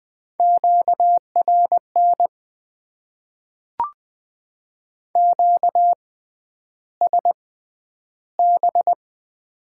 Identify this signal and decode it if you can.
Morse code, heard as a keyed tone, about 20 wpm, spaced as for 10 wpm, 700 Hz. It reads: QRN QSB
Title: Top 90 QSO Elements - 6x Character Spacing - 20wpm